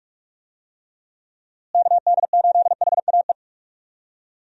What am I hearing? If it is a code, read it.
Morse code, heard as a keyed tone, about 45 wpm, 700 Hz. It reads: KD8HAE
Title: USA Call Signs - 6 Characters - Repeated 3x - 45wpm